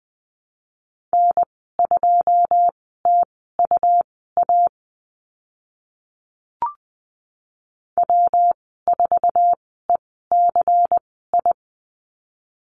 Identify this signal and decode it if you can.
Morse code, heard as a keyed tone, about 20 wpm, spaced as for 14 wpm, 700 Hz. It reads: N2TUA W4ECI
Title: 128 Lesson 21g - Number 6 - Call Signs 20wpm